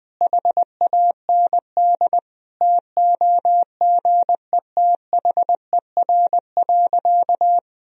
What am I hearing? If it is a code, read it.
Morse code, heard as a keyed tone, about 20 wpm, 700 Hz. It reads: HAND TOGETHER.
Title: Sentences from Top 1000 Words - Review Part 3 - 20wpm